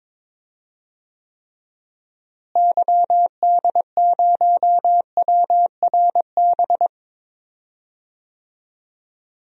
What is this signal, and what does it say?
Morse code, heard as a keyed tone, about 22 wpm, 700 Hz. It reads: YD0WRB
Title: DXCC-100 Call Signs - 6 Characters - Repeated 3x - 22wpm